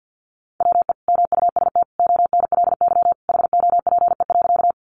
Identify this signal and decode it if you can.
Morse code, heard as a keyed tone, about 50 wpm, 700 Hz. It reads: WE MUST ONLY HOPE.
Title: Sentences from Top 1000 Words 50wpm